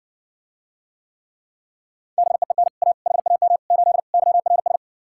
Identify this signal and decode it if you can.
Morse code, heard as a keyed tone, about 60 wpm, 700 Hz. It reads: BEEN A HAM 8 YRS